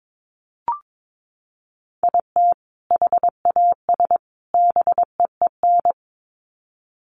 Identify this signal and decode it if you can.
Morse code, heard as a keyed tone, about 22 wpm, 700 Hz. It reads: IT HAS BEEN
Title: Sets of 3 Words 22wpm